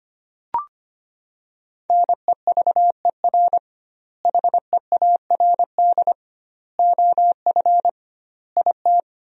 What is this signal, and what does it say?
Morse code, heard as a keyed tone, about 25 wpm, 700 Hz. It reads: NEVER HEARD OF IT
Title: 2x Word-Spacing - Sentences from Top 400 Words - 25wpm